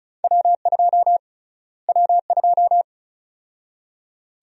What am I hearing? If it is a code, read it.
Morse code, heard as a keyed tone, about 36 wpm, 700 Hz. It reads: W2 W2
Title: USA Call Sign Prefixes - Repeated 2x - 35wpm